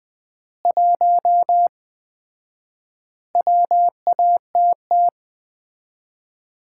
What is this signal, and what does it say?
Morse code, heard as a keyed tone, about 21 wpm, 700 Hz. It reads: 1 WATT
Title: Common QSO Phrases - 4x Word-Spacing - 20wpm